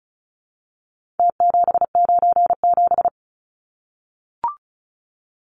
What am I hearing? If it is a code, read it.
Morse code, heard as a keyed tone, about 34 wpm, 700 Hz. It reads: T797
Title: Numbers - Contest Style - 4-Digit 35wpm